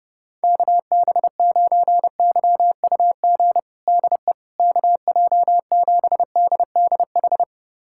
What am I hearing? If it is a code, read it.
Morse code, heard as a keyed tone, about 30 wpm, 700 Hz. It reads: KB9YUG DE KJ7DDH
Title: USA Callsign DE Callsign - 6 Characters - 30wpm